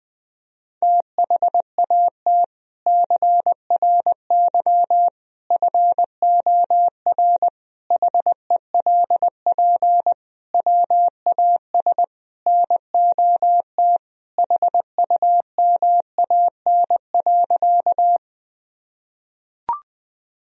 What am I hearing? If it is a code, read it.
Morse code, heard as a keyed tone, about 20 wpm, 700 Hz. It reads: THAT CRY FOR HELP WAS NOT HUMAN.
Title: Sentences from Top 1000 Words - Review Part 3 - 20wpm